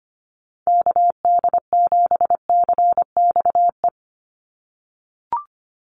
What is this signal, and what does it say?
Morse code, heard as a keyed tone, about 25 wpm, 700 Hz. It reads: KD7CXE
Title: USA Call Signs 25wpm